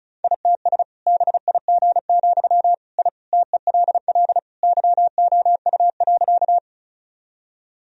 Morse code, heard as a keyed tone, about 36 wpm, 700 Hz. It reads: ITS BIG, I TELL YOU.
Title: Sentences from Top 1000 Words - Review Part 1 - 35wpm